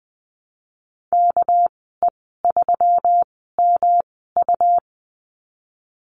Morse code, heard as a keyed tone, about 19 wpm, spaced as for 14 wpm, 700 Hz. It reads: KE3MU
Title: 231 Lesson 37g - Letter x - Call Signs 20wpm